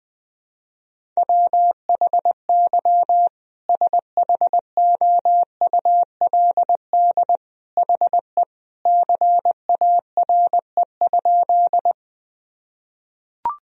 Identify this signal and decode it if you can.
Morse code, heard as a keyed tone, about 21 wpm, 700 Hz. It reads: WHY SHOULD HE CARE?
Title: Sentences from Top 1000 Words - Review Part 1 - 20wpm